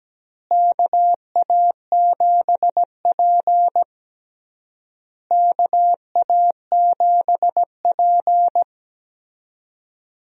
Morse code, heard as a keyed tone, about 17 wpm, 700 Hz. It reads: KA7P KA7P
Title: USA Call Signs - 4 Characters - Repeated 3x - 17wpm